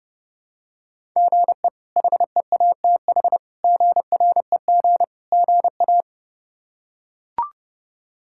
Morse code, heard as a keyed tone, about 32 wpm, 700 Hz. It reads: GE HEATH GREG GA
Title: K1USN SST QSO Practice - 30wpm